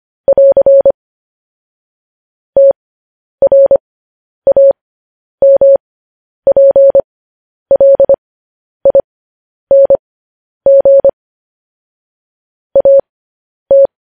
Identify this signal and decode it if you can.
Morse code, heard as a keyed tone, about 25 wpm, spaced as for 10 wpm, 550 Hz. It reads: <AR> TRAMPLING AT
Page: news headlines in morse code at 10 wpm